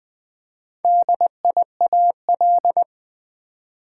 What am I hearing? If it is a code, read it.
Morse code, heard as a keyed tone, about 20 wpm, 700 Hz. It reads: DIAL
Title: Build A Word - 2 Letters At A Time - 8-Letter Words - 20wpm